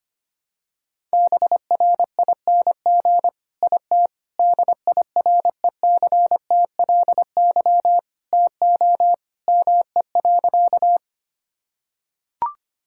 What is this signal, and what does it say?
Morse code, heard as a keyed tone, about 25 wpm, 700 Hz. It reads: BRING IT DIRECTLY TO ME.
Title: Sentences from Top 1000 Words - Review Part 5 - 25wpm